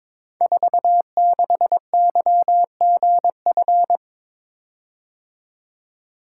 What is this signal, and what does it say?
Morse code, heard as a keyed tone, about 22 wpm, 700 Hz. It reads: 46YGF
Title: Code Groups - 5 Letter:Numbers - 22wpm